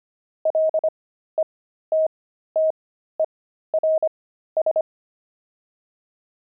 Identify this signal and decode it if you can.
Morse code, heard as a keyed tone, about 25 wpm, spaced as for 13 wpm, 630 Hz. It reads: LETTERS
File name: pangram-fs-13-cs-25-jocks